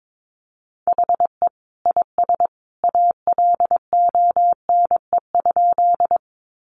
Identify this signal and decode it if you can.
Morse code, heard as a keyed tone, about 22 wpm, 700 Hz. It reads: HE IS ALONE?
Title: Sentences from Top 1000 Words 22wpm